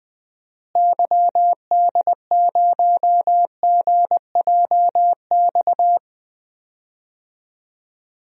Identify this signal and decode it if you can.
Morse code, heard as a keyed tone, about 20 wpm, 700 Hz. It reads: YD0GJX
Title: DXCC-100 Call Signs - 6 Characters - Repeated 2x - 20wpm